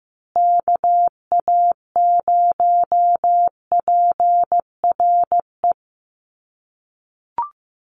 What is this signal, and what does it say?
Morse code, heard as a keyed tone, about 15 wpm, 700 Hz. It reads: KA0PRE
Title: USA Call Signs Repeated 3x 15wpm